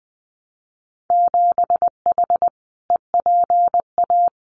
Morse code, heard as a keyed tone, about 19 wpm, 700 Hz. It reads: 7H EPA
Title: Winter Field Day Exchanges Encore 20wpm